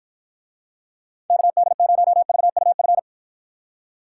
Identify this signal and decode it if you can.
Morse code, heard as a keyed tone, about 52 wpm, 700 Hz. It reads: KD0VUF